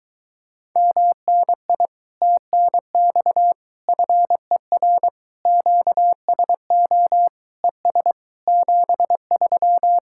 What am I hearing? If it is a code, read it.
Morse code, heard as a keyed tone, about 23 wpm, 700 Hz. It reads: MNI TNX FER QSO ES 73